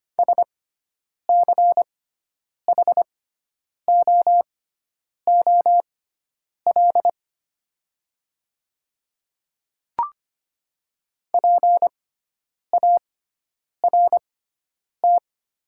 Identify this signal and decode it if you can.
Morse code, heard as a keyed tone, about 25 wpm, spaced as for 9 wpm, 700 Hz. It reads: SCHOOL PART
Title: 6x Character Spacing - Top 300 Words - 25wpm